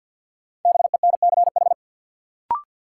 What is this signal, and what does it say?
Morse code, heard as a keyed tone, about 51 wpm, 700 Hz. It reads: BENCH
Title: Sentences from Top 1400 Words - New Word Drill - 50wpm